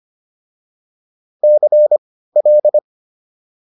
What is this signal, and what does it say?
Morse code, heard as a keyed tone, about 25 wpm, spaced as for 15 wpm, 600 Hz. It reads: CL